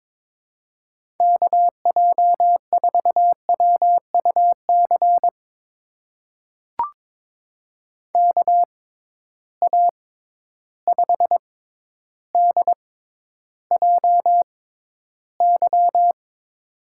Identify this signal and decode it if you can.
Morse code, heard as a keyed tone, about 22 wpm, spaced as for 11 wpm, 700 Hz. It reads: KJ4WUC KA5DJY
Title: USA Call Signs - 6 Characters - 6x Character Spacing - 22wpm